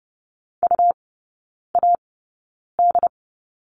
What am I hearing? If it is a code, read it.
Morse code, heard as a keyed tone, about 30 wpm, spaced as for 9 wpm, 690 Hz. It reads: UAD
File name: Lesson 17e - Letter u - Three Characters 30wpm